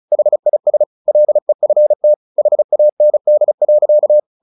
Morse code, heard as a keyed tone, about 34 wpm, 600 Hz. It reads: HIS LEFT HAND.
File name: the old man and the sea / the old man and the sea 1 35wpm